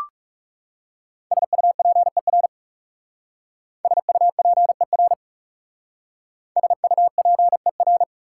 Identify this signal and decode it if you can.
Morse code, heard as a keyed tone, about 37 wpm, 700 Hz. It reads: SUPER SUPER SUPER
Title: Speed-Racing Prefixes and Suffixes 30wpm